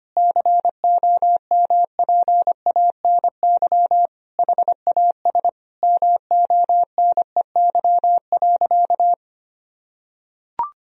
Morse code, heard as a keyed tone, about 25 wpm, 700 Hz. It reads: COMPANY HAS MONEY.
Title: Sentences from Top 1000 Words - Review Part 3 - 25wpm